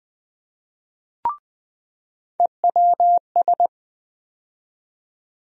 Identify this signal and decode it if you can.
Morse code, heard as a keyed tone, about 20 wpm, 700 Hz. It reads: EWS
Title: USA Call Sign Postfixes - 3 Letters 20wpm